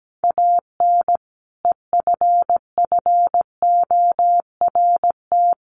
Morse code, heard as a keyed tone, about 17 wpm, 700 Hz. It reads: AN EFFORT
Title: Sentences from Top 1500 Words 17wpm